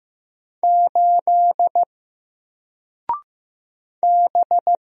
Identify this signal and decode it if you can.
Morse code, heard as a keyed tone, about 15 wpm, 700 Hz. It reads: 8 B
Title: Single Letter-Number 15wpm